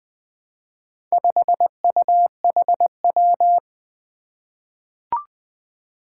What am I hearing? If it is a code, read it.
Morse code, heard as a keyed tone, about 20 wpm, 700 Hz. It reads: 5UHW
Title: Four Challenging Letter-Number Combinations 20wpm